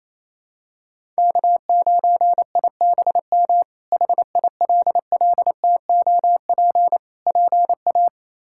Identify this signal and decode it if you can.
Morse code, heard as a keyed tone, about 27 wpm, 700 Hz. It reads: K9IBM HILLTOP PA
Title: K1USN SST QSO Practice - 28wpm